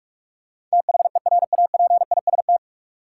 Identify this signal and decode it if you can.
Morse code, heard as a keyed tone, about 46 wpm, 700 Hz. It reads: THERAPIST
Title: Practice QSO - Exchange 3 - More Info - 45wpm